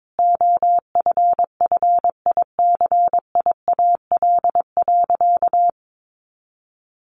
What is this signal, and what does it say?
Morse code, heard as a keyed tone, about 22 wpm, 700 Hz. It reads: OFFICIAL.
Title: Sentences from Top 1000 Words 22wpm